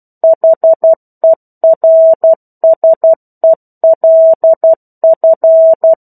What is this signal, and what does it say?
Morse code, heard as a keyed tone, about 12 wpm, 650 Hz. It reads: HERSELF
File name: 01-Chapter-01 12 WPM